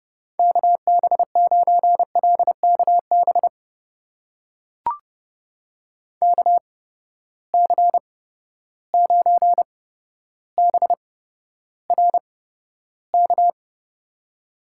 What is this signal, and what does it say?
Morse code, heard as a keyed tone, about 31 wpm, spaced as for 13 wpm, 700 Hz. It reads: KB9LKB KC9BRK
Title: USA Call Signs - 6 Characters - 8x Character Spacing - 30wpm